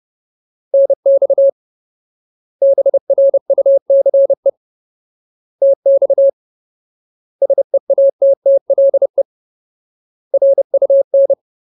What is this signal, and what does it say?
Morse code, heard as a keyed tone, about 30 wpm, 550 Hz. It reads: NX BRUCE TX SEATTLE RUN
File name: Ultimate Word List / 30WPM Ultimate